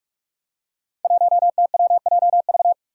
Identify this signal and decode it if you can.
Morse code, heard as a keyed tone, about 42 wpm, 700 Hz. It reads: 1TWJV
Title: Five Challenging Letter-Number Combinations 45wpm